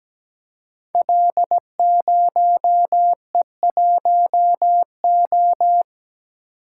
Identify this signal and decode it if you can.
Morse code, heard as a keyed tone, about 17 wpm, 700 Hz. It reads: L0E1O
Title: Code Groups - 5 Letter:Numbers - 17wpm